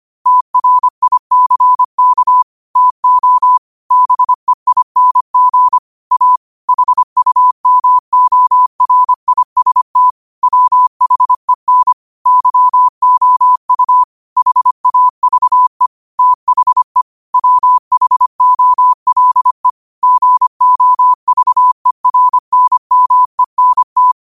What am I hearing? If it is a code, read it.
Morse code, heard as a keyed tone, about 25 wpm, 1000 Hz. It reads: TRICK TO BEING A HUMORIST WHEN YOU HAVE THE WHOLE GOVERNMENT
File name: qotdfiles / 25wpm